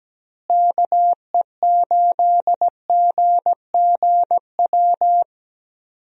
Morse code, heard as a keyed tone, about 17 wpm, 700 Hz. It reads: KE8GGW
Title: USA Call Signs - 6 Characters - Repeated 3x - 17wpm